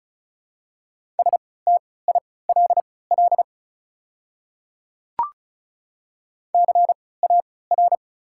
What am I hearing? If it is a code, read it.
Morse code, heard as a keyed tone, about 34 wpm, spaced as for 20 wpm, 700 Hz. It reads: STILL CAR
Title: 3x Character Spacing - Top 300 Words - 35wpm